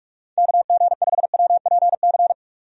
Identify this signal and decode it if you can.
Morse code, heard as a keyed tone, about 42 wpm, 700 Hz. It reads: KG5WPC
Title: USA Call Signs - 6 Characters - 45wpm